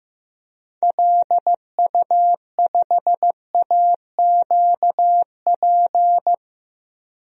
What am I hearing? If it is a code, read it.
Morse code, heard as a keyed tone, about 15 wpm, 700 Hz. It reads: LU5AQP
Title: DXCC-100 Call Signs - 6 Characters - 15wpm